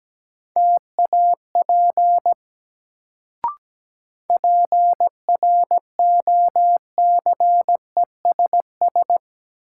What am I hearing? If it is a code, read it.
Morse code, heard as a keyed tone, about 17 wpm, 700 Hz. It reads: TAP PROCESS
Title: Sentences from Top 1400 Words - New Word Drill Alt - 17wpm